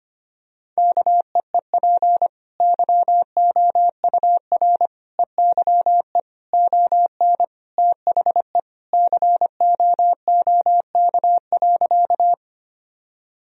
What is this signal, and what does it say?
Morse code, heard as a keyed tone, about 25 wpm, 700 Hz. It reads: KEEP YOUR EYE ON THE COOK.
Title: Sentences from Top 1000 Words - Review Part 4 - 25wpm